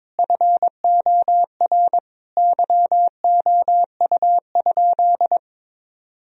Morse code, heard as a keyed tone, about 22 wpm, 700 Hz. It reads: FOR YOU?
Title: Sentences from Top 1000 Words - Review Part 2 - 22wpm